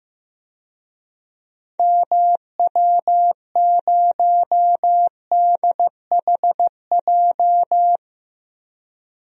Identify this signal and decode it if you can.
Morse code, heard as a keyed tone, about 15 wpm, 700 Hz. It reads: MW0DHJ
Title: DXCC-100 Call Signs - 6 Characters - 15wpm